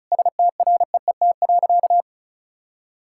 Morse code, heard as a keyed tone, about 34 wpm, 700 Hz. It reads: STREET.
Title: Sentences from Top 1000 Words - Review Part 5 - 35wpm